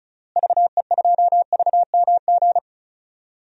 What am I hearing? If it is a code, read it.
Morse code, heard as a keyed tone, about 36 wpm, 700 Hz. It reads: VE2VMG
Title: DXCC-100 Call Signs - 6 Characters - Repeated 2x - 35wpm